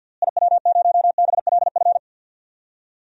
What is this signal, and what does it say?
Morse code, heard as a keyed tone, about 50 wpm, 700 Hz. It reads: IW0BLF
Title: DXCC-100 Call Signs - 6 Characters - Repeated 3x - 50wpm